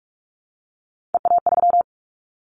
Morse code, heard as a keyed tone, about 44 wpm, 700 Hz. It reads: EA3
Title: Callsign Prefixes for Top 25 DXCC - 45wpm